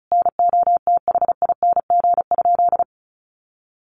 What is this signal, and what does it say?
Morse code, heard as a keyed tone, about 34 wpm, 700 Hz. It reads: NOTHING?
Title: Sentences from Top 1400 Words 35wpm